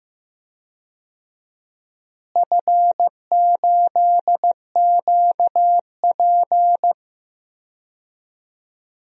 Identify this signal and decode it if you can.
Morse code, heard as a keyed tone, about 15 wpm, 700 Hz. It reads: F8QP
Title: DXCC-100 Call Signs - 4 Characters - Repeated 3x - 15wpm